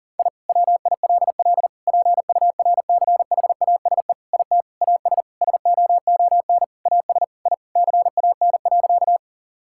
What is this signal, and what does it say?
Morse code, heard as a keyed tone, about 38 wpm, 700 Hz. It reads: I WILL PURCHASE IT AS SOON AS I CAN.
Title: Sentences from Top 1000 Words - Review Part 5 - 40wpm